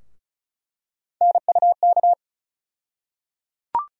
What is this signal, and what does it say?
Morse code, heard as a keyed tone, about 34 wpm, 700 Hz. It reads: NUK
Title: USA Call Sign Postfixes - 3 Letters 35wpm